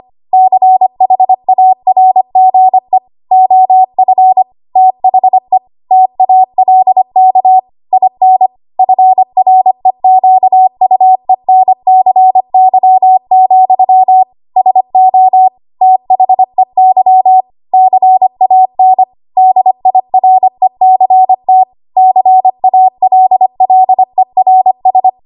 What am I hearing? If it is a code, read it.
Morse code, heard as a keyed tone, about 25 wpm, 750 Hz. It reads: CHARGE OF THE TALK IN FREQUENCY, SO THEY CAN DIRECT CALLERS
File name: Morse / 25WPM